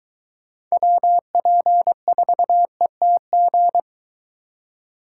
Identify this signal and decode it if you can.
Morse code, heard as a keyed tone, about 23 wpm, 700 Hz. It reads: WP4ETG